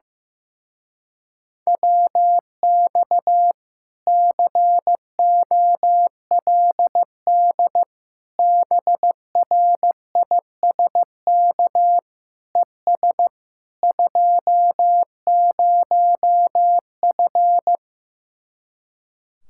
Morse code, heard as a keyed tone, about 15 wpm, 700 Hz. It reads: WX COLD BRISK ES 20F
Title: Speed-Racing Common QSO Phrases Encore - 15wpm